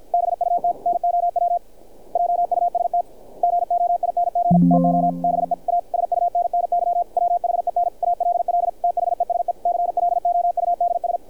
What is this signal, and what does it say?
Morse code, heard as a keyed tone, about 50 wpm, 700 Hz, with radio noise. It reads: BUT NOW JUST GOING TO BE A SUNNY WHEN ALL THESE CLOUDS
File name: archived / CRSnet